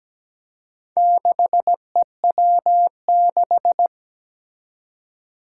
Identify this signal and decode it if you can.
Morse code, heard as a keyed tone, about 17 wpm, 700 Hz. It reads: 6EW6
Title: Four Challenging Letter-Number Combinations 17wpm